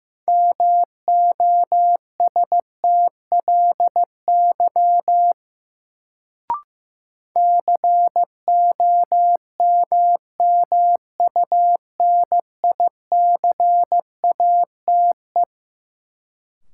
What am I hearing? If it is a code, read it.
Morse code, heard as a keyed tone, about 15 wpm, 700 Hz. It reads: MOSTLY COMMUNICATE
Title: Sentences from Top 1300 Words - New Word Drill Alt - 15wpm